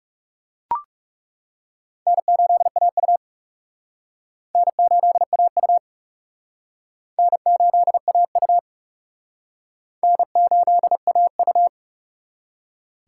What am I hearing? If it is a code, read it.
Morse code, heard as a keyed tone, about 36 wpm, 700 Hz. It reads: N8AU N8AU N8AU N8AU
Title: Speed Racing - USA 1x2 Call Signs 30wpm